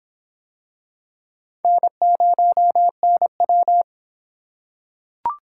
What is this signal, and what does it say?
Morse code, heard as a keyed tone, about 24 wpm, 700 Hz. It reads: N0NW